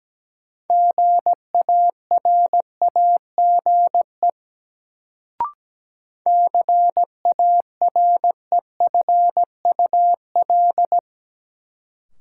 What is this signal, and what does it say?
Morse code, heard as a keyed tone, about 17 wpm, 700 Hz. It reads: GARAGE CAREFUL
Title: Sentences from Top 1600 Words - New Word Drill Alt - 17wpm